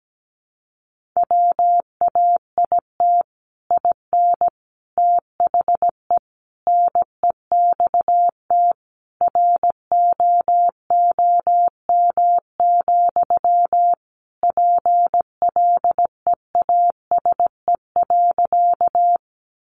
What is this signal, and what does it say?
Morse code, heard as a keyed tone, about 17 wpm, 700 Hz. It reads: WAIT IN THE NEXT ROOM, PLEASE.
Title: Sentences from Top 1500 Words 17wpm